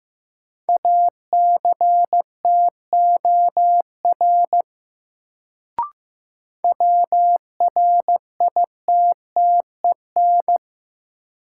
Morse code, heard as a keyed tone, about 15 wpm, 700 Hz. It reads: ACTOR WRITTEN
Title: Sentences from Top 1500 Words - New Word Drill Alt - 15wpm